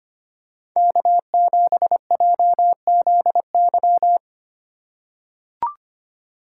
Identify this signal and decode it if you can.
Morse code, heard as a keyed tone, about 25 wpm, 700 Hz. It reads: K7JZY
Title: USA Call Signs - 2x Character Spacing - 25wpm